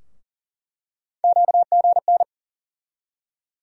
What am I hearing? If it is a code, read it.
Morse code, heard as a keyed tone, about 40 wpm, 700 Hz. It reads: QGN
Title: USA Call Sign Postfixes - 3 Letters 40wpm